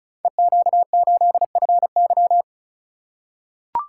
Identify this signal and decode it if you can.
Morse code, heard as a keyed tone, about 36 wpm, 700 Hz. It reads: EQ8FY
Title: Code Groups - 5 Letter:Numbers - 35wpm